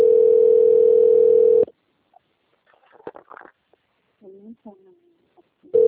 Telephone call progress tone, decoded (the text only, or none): ringback tone